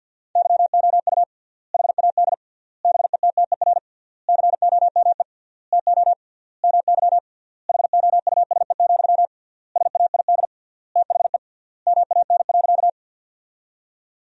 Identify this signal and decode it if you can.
Morse code, heard as a keyed tone, about 50 wpm, 700 Hz. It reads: YOU HAD BETTER COME TO MY HOUSE, SAID THE MAN.
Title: 3x Word-Spacing - Sentences from Top 400 Words - 50wpm